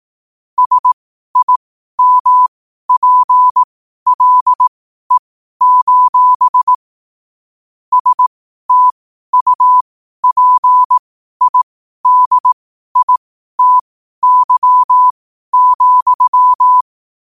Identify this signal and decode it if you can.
Morse code, heard as a keyed tone, about 18 wpm, spaced as for 12 wpm, 1000 Hz. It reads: SIMPLE: STUPIDITY,
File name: qotdfiles / 12wpm